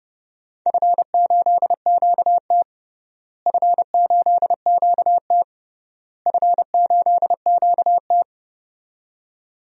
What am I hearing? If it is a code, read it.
Morse code, heard as a keyed tone, about 32 wpm, 700 Hz. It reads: F8QT F8QT F8QT
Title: DXCC-100 Call Signs - 4 Characters - Repeated 3x - 30wpm